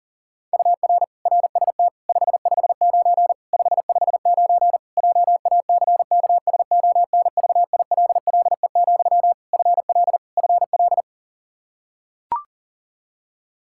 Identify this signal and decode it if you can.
Morse code, heard as a keyed tone, about 40 wpm, 700 Hz. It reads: UR RST 559 559 JACKSONVILLE, FL FL
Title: Practice QSO - Exchange 1 - Beginning - 40wpm